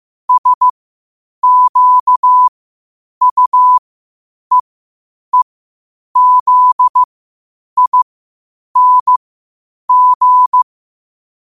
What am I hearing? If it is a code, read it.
Morse code, heard as a keyed tone, about 15 wpm, spaced as for 8 wpm, 1000 Hz. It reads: SQUEEZING
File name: qotdfiles / 8wpm